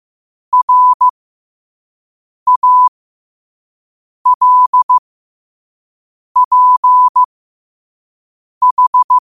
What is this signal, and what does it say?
Morse code, heard as a keyed tone, about 15 wpm, spaced as for 5 wpm, 1000 Hz. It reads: RALPH
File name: qotdfiles / 5wpm